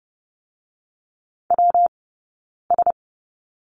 Morse code, heard as a keyed tone, about 30 wpm, 700 Hz. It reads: W S
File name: Lesson 19d - Letter w - Two Characters 30wpm